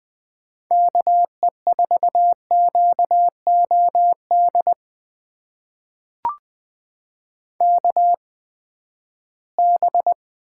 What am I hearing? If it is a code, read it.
Morse code, heard as a keyed tone, about 20 wpm, spaced as for 14 wpm, 700 Hz. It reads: KE4QOD KB
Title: USA Call Signs - 6 Characters - 8x Character Spacing - 20wpm